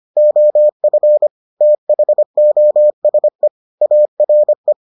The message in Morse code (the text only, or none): OF THOSE ARE